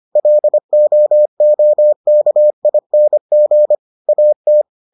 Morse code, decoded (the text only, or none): LOOKING AT